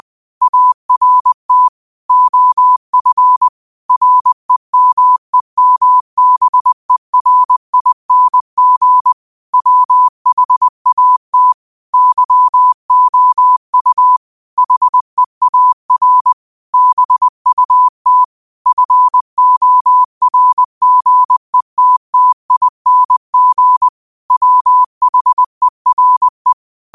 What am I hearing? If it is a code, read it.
Morse code, heard as a keyed tone, about 20 wpm, 1000 Hz. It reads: ART OF REMEMBERING WHAT YOU HEAR BUT FORGETTING WHERE